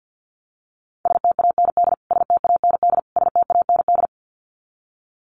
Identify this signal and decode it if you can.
Morse code, heard as a keyed tone, about 50 wpm, 700 Hz. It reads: STAND STAND STAND
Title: Top 500 Words Repeated 50wpm